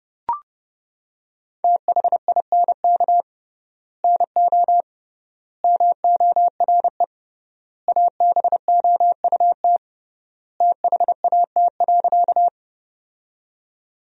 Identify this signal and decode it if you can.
Morse code, heard as a keyed tone, about 30 wpm, 700 Hz. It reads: THINK NO MORE ABOUT THAT.
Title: 3x Word-Spacing - Sentences from Top 200 Words - 30wpm